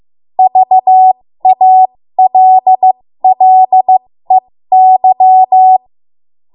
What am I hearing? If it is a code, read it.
Morse code, heard as a keyed tone, about 15 wpm, spaced as for 13 wpm, 750 Hz. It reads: VALLEY
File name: Morse / 13WPM